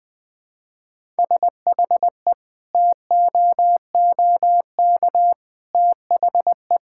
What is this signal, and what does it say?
Morse code, heard as a keyed tone, about 20 wpm, 700 Hz. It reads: SHE TOOK THE
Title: Sentences from Top 1000 Words - Review Part 5 - 20wpm